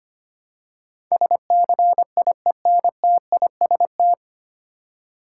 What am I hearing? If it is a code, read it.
Morse code, heard as a keyed tone, about 25 wpm, 700 Hz. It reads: SCIENTIST
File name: practice / 25wpm / 9 Letter Words - First 3 Letters Given - 25wpm